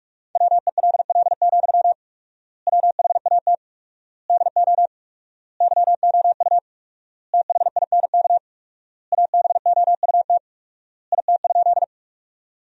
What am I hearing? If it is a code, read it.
Morse code, heard as a keyed tone, about 46 wpm, 700 Hz. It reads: WELL, WHAT DO YOU THINK ABOUT IT?
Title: 4x Word-Spacing - Sentences from Top 200 Words - 45wpm